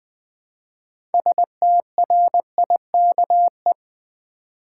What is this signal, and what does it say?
Morse code, heard as a keyed tone, about 20 wpm, 700 Hz. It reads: STRIKE
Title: 6 Letter Words - First 3 Letters Given - 20wpm